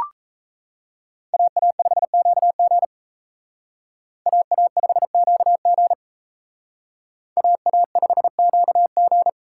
Morse code, heard as a keyed tone, about 35 wpm, 700 Hz. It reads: AA5QG AA5QG AA5QG
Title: Speed Racing - USA Call Signs 28wpm